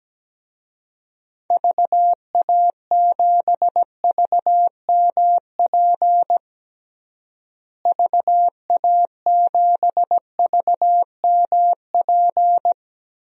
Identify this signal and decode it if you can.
Morse code, heard as a keyed tone, about 17 wpm, 700 Hz. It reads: VA7VMP VA7VMP
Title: DXCC-100 Call Signs - 6 Characters - Repeated 2x - 17wpm